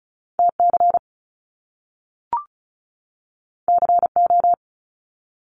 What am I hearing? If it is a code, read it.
Morse code, heard as a keyed tone, about 36 wpm, 700 Hz. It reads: T C C O
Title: Top Two Letter Combinations in Words 35wpm